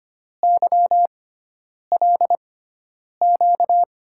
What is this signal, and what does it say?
Morse code, heard as a keyed tone, about 25 wpm, spaced as for 9 wpm, 700 Hz. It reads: YLQ
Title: USA Call Signs - 6x Character Spacing - 25wpm